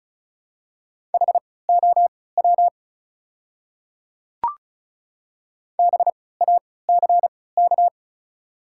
Morse code, heard as a keyed tone, about 36 wpm, spaced as for 20 wpm, 700 Hz. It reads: HOW BACK
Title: 3x Character Spacing - Top 300 Words - 35wpm